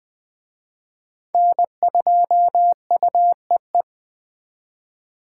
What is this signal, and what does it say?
Morse code, heard as a keyed tone, about 21 wpm, 700 Hz. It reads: N2UEE
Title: Speed Racing - USA 1x3 Call Signs 17wpm